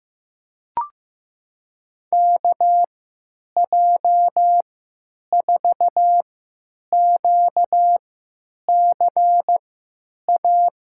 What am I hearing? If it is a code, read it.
Morse code, heard as a keyed tone, about 15 wpm, spaced as for 9 wpm, 700 Hz. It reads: KJ4QCA
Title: USA Call Signs - 6 Characters - 3x Character Spacing - 15wpm